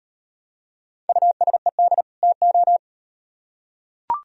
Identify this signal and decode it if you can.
Morse code, heard as a keyed tone, about 35 wpm, 700 Hz. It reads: USED TO